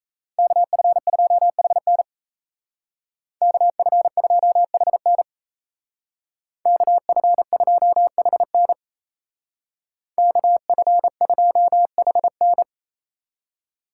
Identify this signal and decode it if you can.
Morse code, heard as a keyed tone, about 36 wpm, 700 Hz. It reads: KF2HN KF2HN KF2HN KF2HN